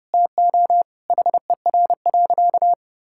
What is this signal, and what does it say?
Morse code, heard as a keyed tone, about 28 wpm, 700 Hz. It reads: TO HER.
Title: Sentences from Top 1000 Words - Review Part 5 - 30wpm